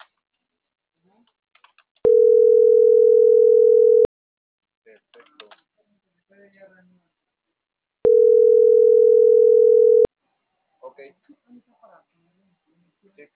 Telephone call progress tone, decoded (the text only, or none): ringback tone